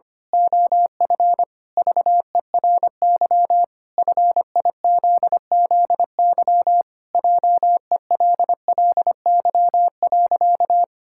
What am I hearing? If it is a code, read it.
Morse code, heard as a keyed tone, about 25 wpm, 700 Hz. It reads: OF VERY FIZZY JELLY.
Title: Pangram Challenge v2 - 25wpm